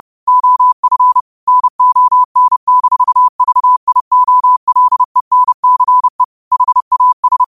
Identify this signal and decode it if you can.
Morse code, heard as a keyed tone, about 30 wpm, 1000 Hz. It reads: OF NON-VIOLENCE HAS
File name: qotdfiles / 30wpm